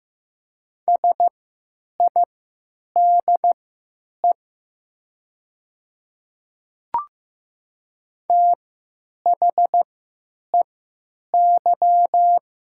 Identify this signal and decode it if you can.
Morse code, heard as a keyed tone, about 15 wpm, spaced as for 9 wpm, 700 Hz. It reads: SIDE THEY
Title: 3x Character Spacing - Top 100 Words - 15wpm